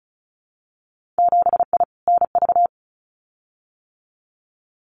Morse code, heard as a keyed tone, about 36 wpm, 700 Hz. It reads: 7I NV
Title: Winter Field Day Exchanges Encore 35wpm